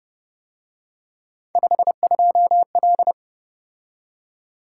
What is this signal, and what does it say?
Morse code, heard as a keyed tone, about 30 wpm, 700 Hz. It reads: 52L